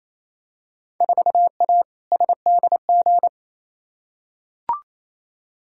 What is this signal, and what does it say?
Morse code, heard as a keyed tone, about 28 wpm, 700 Hz. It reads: 4A SDG
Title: Field Day Exchanges Encore - 28wpm